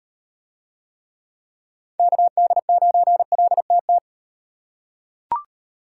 Morse code, heard as a keyed tone, about 34 wpm, 700 Hz. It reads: KD9LTT